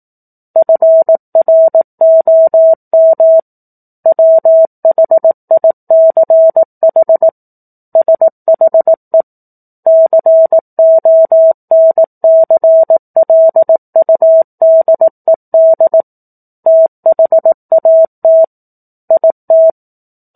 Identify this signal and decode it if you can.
Morse code, heard as a keyed tone, about 18 wpm, 650 Hz. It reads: FROM WHICH SHE CONCLUDED THAT IT
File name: 04-Chapter-04 18 WPM